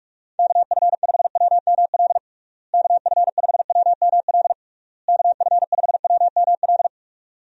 Morse code, heard as a keyed tone, about 46 wpm, 700 Hz. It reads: KF5WML KF5WML KF5WML
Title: USA Call Signs - 6 Characters - Repeated 3x - 45wpm